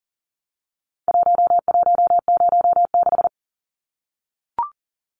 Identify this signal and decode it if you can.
Morse code, heard as a keyed tone, about 40 wpm, 700 Hz. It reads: 1106
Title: Numbers - Contest Style - 4-Digit 40wpm